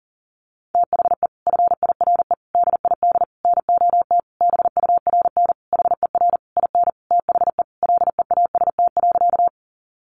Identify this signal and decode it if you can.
Morse code, heard as a keyed tone, about 37 wpm, 700 Hz. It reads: THE FIRE DID NOT BURN HER IN THE LEAST.
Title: Sentences from Top 1300 Words 40wpm